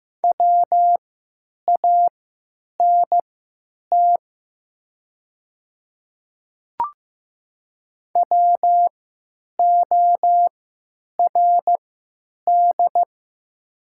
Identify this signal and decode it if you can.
Morse code, heard as a keyed tone, about 15 wpm, spaced as for 9 wpm, 700 Hz. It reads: WANT WORD
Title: 3x Character Spacing - Top 200 Words - 15wpm